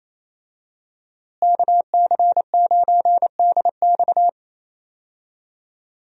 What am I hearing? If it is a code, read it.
Morse code, heard as a keyed tone, about 28 wpm, 700 Hz. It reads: KC9DX
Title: USA Call Signs - 5 Characters - Repeated 2x - 28wpm